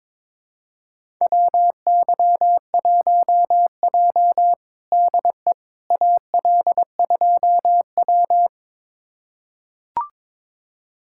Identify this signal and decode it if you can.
Morse code, heard as a keyed tone, about 22 wpm, 700 Hz. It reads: WY1J DE AL2W
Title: USA Callsign DE Callsign - 4 Characters - 22wpm